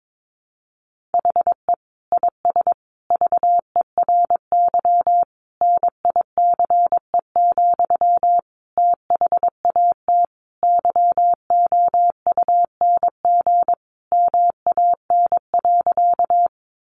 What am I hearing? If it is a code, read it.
Morse code, heard as a keyed tone, about 22 wpm, 700 Hz. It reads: HE IS VERY NICE, THAT YOUNG MAN.
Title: Sentences from Top 1100 Words 22wpm